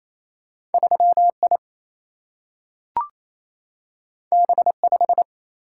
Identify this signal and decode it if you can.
Morse code, heard as a keyed tone, about 28 wpm, 700 Hz. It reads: 3I B5
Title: Single-Letter Single-Number 28wpm